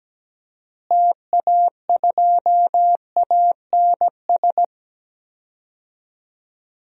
Morse code, heard as a keyed tone, about 17 wpm, 700 Hz. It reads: TA2ANS
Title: DXCC-100 Call Signs - 6 Characters - Repeated 3x - 17wpm